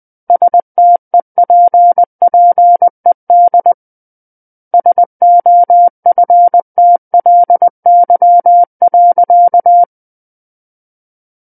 Morse code, heard as a keyed tone, about 20 wpm, 700 Hz. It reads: STEPPED SOFTLY.
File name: 20 WPM